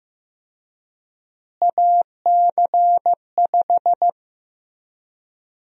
Morse code, heard as a keyed tone, about 15 wpm, 700 Hz. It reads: AC5
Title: USA Call Sign Prefixes - Repeated 2x - 15wpm